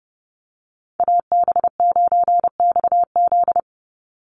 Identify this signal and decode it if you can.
Morse code, heard as a keyed tone, about 30 wpm, 700 Hz. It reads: AB9XZ
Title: USA Call Signs 30wpm